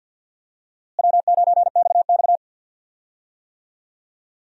Morse code, heard as a keyed tone, about 51 wpm, 700 Hz. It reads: W9XX
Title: USA Call Signs - 4 Characters - Repeated 3x - 50wpm